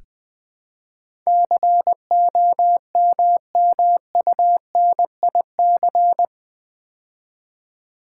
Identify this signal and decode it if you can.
Morse code, heard as a keyed tone, about 20 wpm, 700 Hz. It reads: COMMUNIC
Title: Build A Word - 4 Letters At A Time - 11-Letter Words - 20wpm